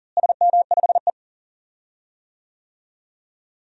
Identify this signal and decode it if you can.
Morse code, heard as a keyed tone, about 40 wpm, 700 Hz. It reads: SM5E
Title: DXCC-100 Call Signs - 4 Characters - Repeated 2x - 40wpm